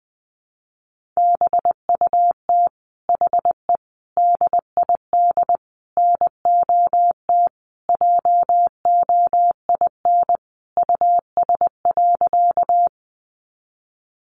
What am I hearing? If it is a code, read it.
Morse code, heard as a keyed tone, about 20 wpm, 700 Hz. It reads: BUT HE DID NOT JOIN US.
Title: Sentences from Top 1400 Words 20wpm